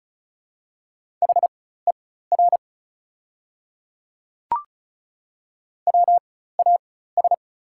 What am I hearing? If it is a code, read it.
Morse code, heard as a keyed tone, about 34 wpm, spaced as for 16 wpm, 690 Hz. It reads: HER WAS
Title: 4x Character Spacing - Top 100 Words - 35wpm